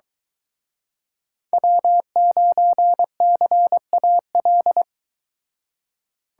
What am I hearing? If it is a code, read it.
Morse code, heard as a keyed tone, about 23 wpm, 700 Hz. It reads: W9CAL